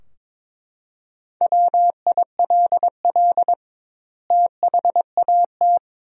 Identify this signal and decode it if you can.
Morse code, heard as a keyed tone, about 22 wpm, 700 Hz. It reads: WILL THAT
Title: 2x Word-Spacing - Sentences from Top 100 Words - 22wpm